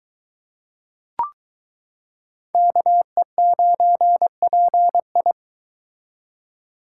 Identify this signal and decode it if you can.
Morse code, heard as a keyed tone, about 22 wpm, 700 Hz. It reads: KE9PI